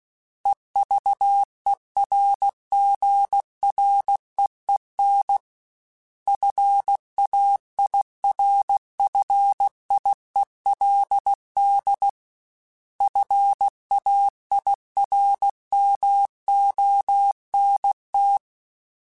Morse code, heard as a keyed tone, about 16 wpm, 800 Hz. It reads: EVERGREEN FAIRFIELD FAIRMONT
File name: fromSMs / FromSMs mp3s / Cities@15